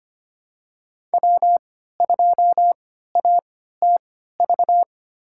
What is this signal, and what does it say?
Morse code, heard as a keyed tone, about 25 wpm, spaced as for 14 wpm, 700 Hz. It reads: W2ATV
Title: USA Call Signs - 5 Characters - 3x Character Spacing - 25wpm